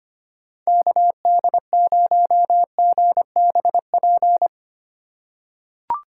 Morse code, heard as a keyed tone, about 25 wpm, 700 Hz. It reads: KD0GBP
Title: USA Call Signs - 6x Character Spacing - 25wpm